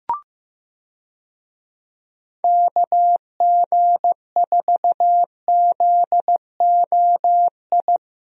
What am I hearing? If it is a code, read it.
Morse code, heard as a keyed tone, about 15 wpm, 700 Hz. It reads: KG4ZOI
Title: USA Call Signs - 6 Characters - 15wpm